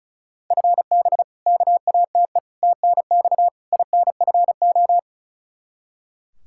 Morse code, heard as a keyed tone, about 34 wpm, 700 Hz. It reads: FB KATE TNX INFO
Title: Practice QSO - Exchange 3 - More Info - 35wpm